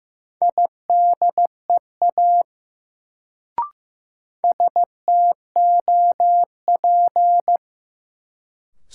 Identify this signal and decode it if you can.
Morse code, heard as a keyed tone, about 15 wpm, 700 Hz. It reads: IDEA STOP
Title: Sentences from Top 300 Words - New Word Drill - 15wpm